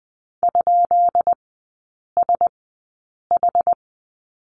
Morse code, heard as a keyed tone, about 20 wpm, spaced as for 8 wpm, 700 Hz. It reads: ?SH
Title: 178 Lesson 29e - Slash - Three Characters 20wpm